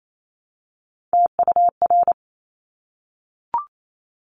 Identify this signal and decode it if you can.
Morse code, heard as a keyed tone, about 28 wpm, 700 Hz. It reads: TUR
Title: Top Three Letter Combinations in Words 28wpm